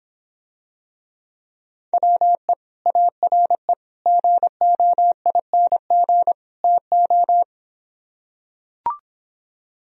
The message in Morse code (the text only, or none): WE ARE GOING TO